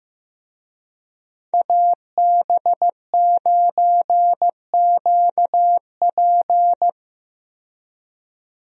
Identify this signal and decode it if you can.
Morse code, heard as a keyed tone, about 15 wpm, 700 Hz. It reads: AB9QP
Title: USA Call Signs - 5 Characters - 15wpm